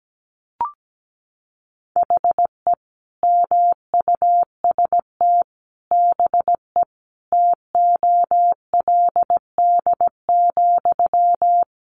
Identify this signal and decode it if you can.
Morse code, heard as a keyed tone, about 17 wpm, 700 Hz. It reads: HE MUST BE TOLD,
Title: Sentences from Top 1400 Words 17wpm